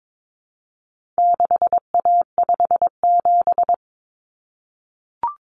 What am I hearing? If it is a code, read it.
Morse code, heard as a keyed tone, about 22 wpm, 700 Hz. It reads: 6A57
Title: Common Cut Numbers - 4-Digit 22wpm